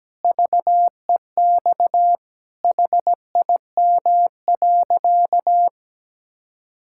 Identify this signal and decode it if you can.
Morse code, heard as a keyed tone, about 17 wpm, 700 Hz. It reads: VEX HIM.
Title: Pangram Challenge v2 - 17wpm